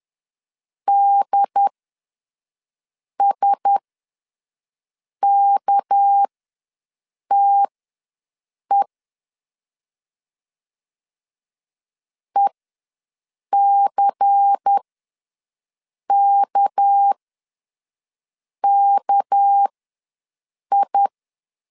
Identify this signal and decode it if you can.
Morse code, heard as a keyed tone, about 10 wpm, spaced as for 5 wpm, 800 Hz. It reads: DSKTE ECKKI
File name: LEZIONI / 6 PKX0 / 6 PK ws 10wpm 3745